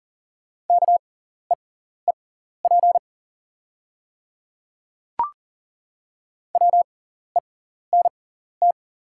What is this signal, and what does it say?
Morse code, heard as a keyed tone, about 37 wpm, spaced as for 14 wpm, 700 Hz. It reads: KEEP WENT
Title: 6x Character Spacing - Top 300 Words - 40wpm